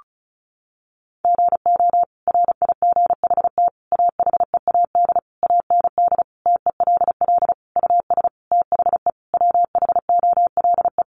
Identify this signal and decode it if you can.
Morse code, heard as a keyed tone, about 34 wpm, 700 Hz. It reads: GO RIGHT AHEAD AND TELL US THE WHOLE
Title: Sentences from Top 1100 Words 35wpm